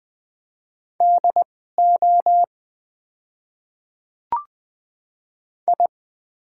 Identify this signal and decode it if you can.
Morse code, heard as a keyed tone, about 20 wpm, spaced as for 15 wpm, 690 Hz. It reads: DO I
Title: 2x Character Spacing - Top 500 Words - 20wpm